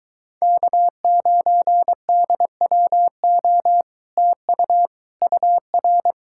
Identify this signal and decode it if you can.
Morse code, heard as a keyed tone, about 23 wpm, 700 Hz. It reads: K9DWO TU UR